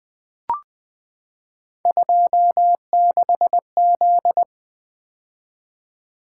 Morse code, heard as a keyed tone, about 20 wpm, 700 Hz. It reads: 26Z
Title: Three Letter-Number Combinations - 20wpm